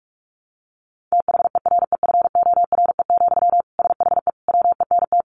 Morse code, heard as a keyed tone, about 44 wpm, 700 Hz. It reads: THEREFORE, SHE WENT
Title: Sentences from Top 800 Words 45wpm